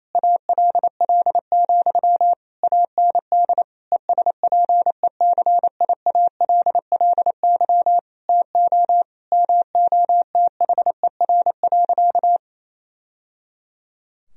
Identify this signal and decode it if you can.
Morse code, heard as a keyed tone, about 28 wpm, 700 Hz. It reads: ALL, AND ESPECIALLY TO MOTHER.
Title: Sentences from Top 1000 Words - Review Part 4 - 28wpm